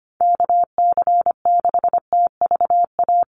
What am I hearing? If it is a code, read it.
Morse code, heard as a keyed tone, about 25 wpm, 700 Hz. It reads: KC6TVA
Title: USA Call Signs Repeated 3x 25wpm